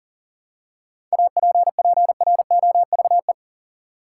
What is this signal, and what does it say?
Morse code, heard as a keyed tone, about 40 wpm, 700 Hz. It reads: APPROVE
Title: Seven Letter Words - 40wpm